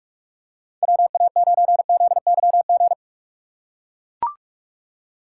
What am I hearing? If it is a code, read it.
Morse code, heard as a keyed tone, about 43 wpm, 700 Hz. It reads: WA9ZYG
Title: USA Call Signs - 2x Character Spacing - 45wpm